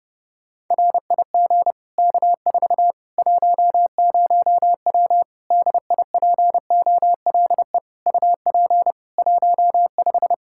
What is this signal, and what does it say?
Morse code, heard as a keyed tone, about 30 wpm, 700 Hz. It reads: RIG K4 10W DIPOLE UP 15
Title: Practice QSO - Exchange 2 - Info - 30wpm